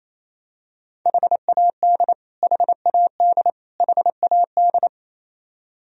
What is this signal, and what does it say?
Morse code, heard as a keyed tone, about 28 wpm, 700 Hz. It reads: HAD HAD HAD
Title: Top 100 Words Repeated 28wpm